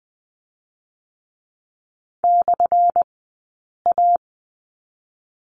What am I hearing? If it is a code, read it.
Morse code, heard as a keyed tone, about 20 wpm, 700 Hz. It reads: / A
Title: 183 Lesson 30d - Letter b - Two Characters 20wpm